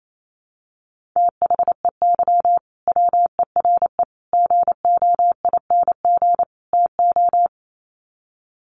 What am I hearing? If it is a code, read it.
Morse code, heard as a keyed tone, about 28 wpm, 700 Hz. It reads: THEY WERE GOING TO
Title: Sets of 4 Words 28wpm